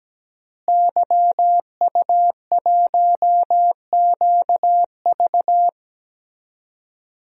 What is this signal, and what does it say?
Morse code, heard as a keyed tone, about 17 wpm, 700 Hz. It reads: YU1QV
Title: DXCC-100 Call Signs - 5 Characters - Repeated 2x - 17wpm